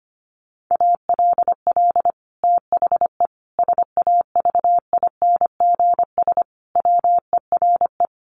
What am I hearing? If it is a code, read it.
Morse code, heard as a keyed tone, about 25 wpm, 700 Hz. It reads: ALL THE SAVINGS WERE
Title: Sentences from Top 1200 Words 25wpm